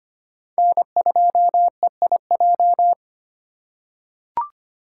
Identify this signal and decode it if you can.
Morse code, heard as a keyed tone, about 25 wpm, 700 Hz. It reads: N2EIJ
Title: USA Call Signs - 5 Characters - 3x Character Spacing - 25wpm